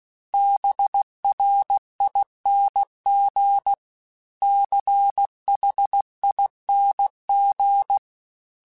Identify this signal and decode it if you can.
Morse code, heard as a keyed tone, about 16 wpm, 800 Hz. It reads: BRING CHING
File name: Special INGs / Special INGs-mp3s / Sngl ing @15